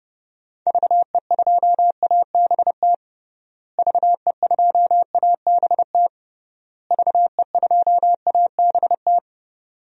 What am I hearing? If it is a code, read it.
Morse code, heard as a keyed tone, about 30 wpm, 700 Hz. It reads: VE2ABT VE2ABT VE2ABT
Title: DXCC-100 Call Signs - 6 Characters - Repeated 3x - 30wpm